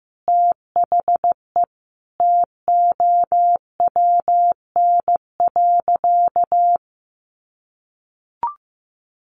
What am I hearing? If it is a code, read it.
Morse code, heard as a keyed tone, about 15 wpm, 700 Hz. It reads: THE TOWN.
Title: Sentences from Top 1400 Words 15wpm